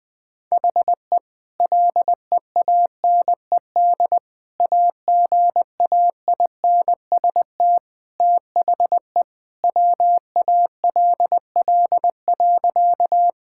Morse code, heard as a keyed tone, about 20 wpm, 700 Hz. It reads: HE LEANED AGAINST THE WALL.
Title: Sentences with 2-Word Phrases - 20wpm